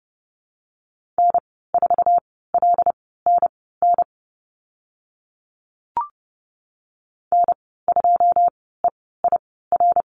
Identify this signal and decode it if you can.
Morse code, heard as a keyed tone, about 30 wpm, spaced as for 17 wpm, 700 Hz. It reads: N4LNN N2EIR
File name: Lesson 14g - Number 2 - Call Signs 30wpm